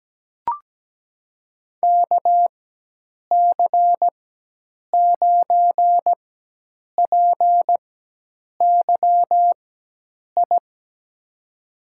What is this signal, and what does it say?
Morse code, heard as a keyed tone, about 17 wpm, spaced as for 8 wpm, 700 Hz. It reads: KC9PYI
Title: USA Call Signs - 4x Character Spacing - 17wpm